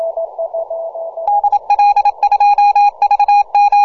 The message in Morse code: DL2VM